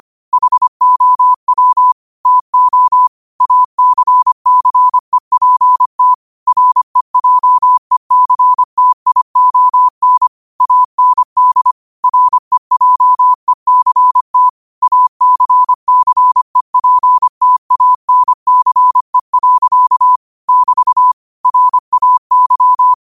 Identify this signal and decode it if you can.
Morse code, heard as a keyed tone, about 25 wpm, 1000 Hz. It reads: HOW TO ACCEPT REJECTION AND REJECT ACCEPTANCE. <BT> RAY
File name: qotdfiles / 25wpm